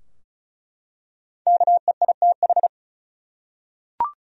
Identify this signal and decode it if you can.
Morse code, heard as a keyed tone, about 36 wpm, 700 Hz. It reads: KEITH
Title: USA Common Names - 35wpm